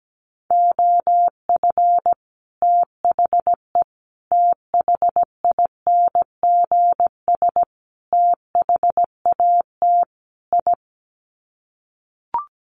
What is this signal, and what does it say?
Morse code, heard as a keyed tone, about 17 wpm, 700 Hz. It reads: OF THE THINGS THAT I
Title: Sets of 5 Words Encore 17wpm